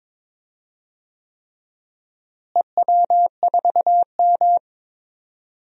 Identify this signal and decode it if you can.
Morse code, heard as a keyed tone, about 22 wpm, 700 Hz. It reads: EW4M
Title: DXCC-100 Call Signs - 4 Characters - Repeated 3x - 22wpm